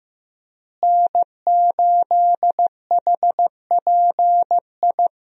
Morse code, heard as a keyed tone, about 15 wpm, 700 Hz. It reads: N8HPI
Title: USA Call Signs - 5 Characters - Repeated 2x - 15wpm